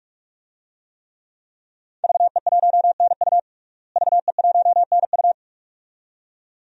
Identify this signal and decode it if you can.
Morse code, heard as a keyed tone, about 46 wpm, 700 Hz. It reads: VE1NU VE1NU
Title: DXCC-100 Call Signs - 5 Characters - Repeated 2x - 45wpm